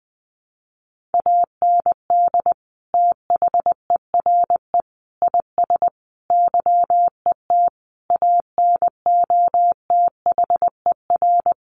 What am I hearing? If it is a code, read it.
Morse code, heard as a keyed tone, about 20 wpm, 700 Hz. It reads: AND THERE IS YET ANOTHER
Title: Sentences from Top 1500 Words 20wpm